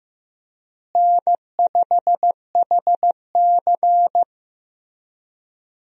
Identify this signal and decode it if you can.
Morse code, heard as a keyed tone, about 15 wpm, 700 Hz. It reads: N5HC
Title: USA Call Signs - 4 Characters - Repeated 3x - 15wpm